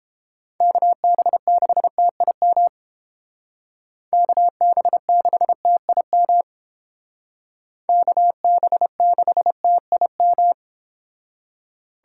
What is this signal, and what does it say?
Morse code, heard as a keyed tone, about 30 wpm, 700 Hz. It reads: KB6TIM KB6TIM KB6TIM